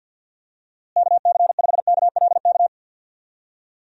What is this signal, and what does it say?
Morse code, heard as a keyed tone, about 50 wpm, 700 Hz. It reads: KC5KLK
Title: USA Call Signs - 6 Characters - Repeated 2x - 50wpm